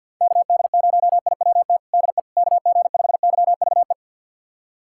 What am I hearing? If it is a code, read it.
Morse code, heard as a keyed tone, about 48 wpm, 700 Hz. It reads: KD0IWT DE KG5YVE
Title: USA Callsign DE Callsign - 6 Characters - 50wpm